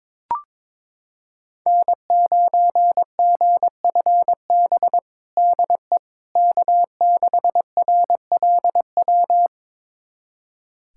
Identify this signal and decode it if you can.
Morse code, heard as a keyed tone, about 22 wpm, 700 Hz. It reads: N9GFB DE K6RLW
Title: USA Callsign DE Callsign - 5 Characters - 22wpm